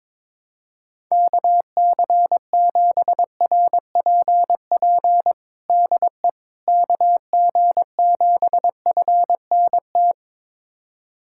Morse code, heard as a keyed tone, about 21 wpm, 700 Hz. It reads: KC7RPP DE KG7FNT
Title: USA Callsign DE Callsign - 6 Characters - 22wpm